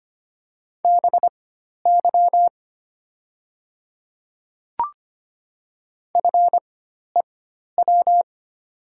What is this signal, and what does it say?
Morse code, heard as a keyed tone, about 25 wpm, spaced as for 12 wpm, 700 Hz. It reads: BY FEW
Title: 4x Character Spacing - Top 300 Words - 25wpm